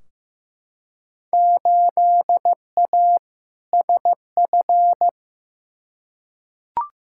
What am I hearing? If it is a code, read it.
Morse code, heard as a keyed tone, about 15 wpm, 700 Hz. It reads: 8A SF
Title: Field Day Exchanges Encore - 15wpm